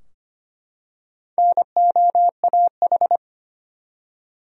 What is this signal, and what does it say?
Morse code, heard as a keyed tone, about 25 wpm, 700 Hz. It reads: NOAH
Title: USA Common Names - Max 4 Letters - 25wpm